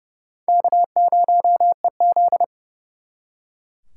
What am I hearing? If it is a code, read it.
Morse code, heard as a keyed tone, about 36 wpm, 700 Hz. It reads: K0EZ
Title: Speed Racing - USA 1x2 Call Signs 30wpm